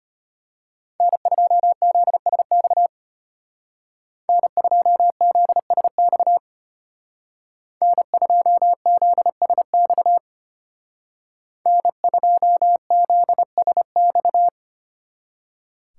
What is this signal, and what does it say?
Morse code, heard as a keyed tone, about 33 wpm, 700 Hz. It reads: N2ZSX N2ZSX N2ZSX N2ZSX